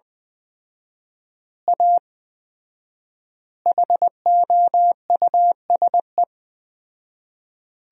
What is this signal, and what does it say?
Morse code, heard as a keyed tone, about 20 wpm, 700 Hz. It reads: A HOUSE
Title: 4x Word-Spacing - Sentences from Top 200 Words - 20wpm